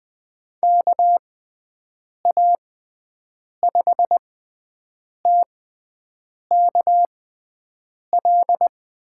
Morse code, heard as a keyed tone, about 20 wpm, spaced as for 7 wpm, 700 Hz. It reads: KA5TKL
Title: USA Call Signs - 6x Character Spacing - 20wpm